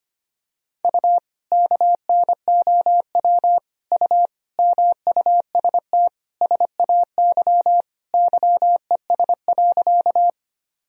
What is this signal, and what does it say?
Morse code, heard as a keyed tone, about 25 wpm, 700 Hz. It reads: U KNOW U MUST SAY YES.
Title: Sentences from Top 500 Words and Abbreviations 25wpm